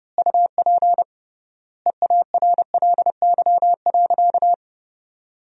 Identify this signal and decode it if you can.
Morse code, heard as a keyed tone, about 30 wpm, 700 Hz. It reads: UP EARLY.
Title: 3x Word-Spacing - Sentences from Top 400 Words - 30wpm